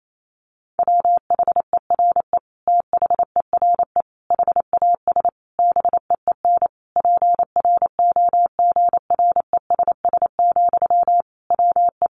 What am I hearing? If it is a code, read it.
Morse code, heard as a keyed tone, about 28 wpm, 700 Hz. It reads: WHERE THERE HAS BEEN PROGRESS, WE
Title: Sentences from Top 1400 Words 28wpm